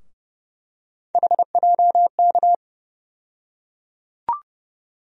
Morse code, heard as a keyed tone, about 30 wpm, 700 Hz. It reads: HJK
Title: USA Call Sign Postfixes - 3 Letters 30wpm